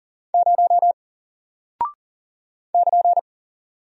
Morse code, heard as a keyed tone, about 38 wpm, 700 Hz. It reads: 0 <KN>
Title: Single Character (Letter:Number:Punctuation:Prosign) 40wpm